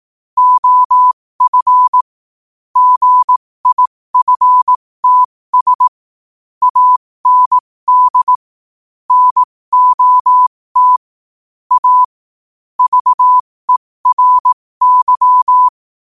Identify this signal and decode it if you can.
Morse code, heard as a keyed tone, about 18 wpm, spaced as for 15 wpm, 1000 Hz. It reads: OF GIFTS AND NOT A VERY
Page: New quotes every day in morse code at 15 Words per minute.